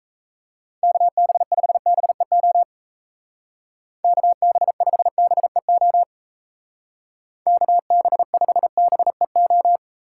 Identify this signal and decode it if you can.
Morse code, heard as a keyed tone, about 35 wpm, 700 Hz. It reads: KB5BEO KB5BEO KB5BEO
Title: Speed Racing - USA 2x3 Call Signs 28wpm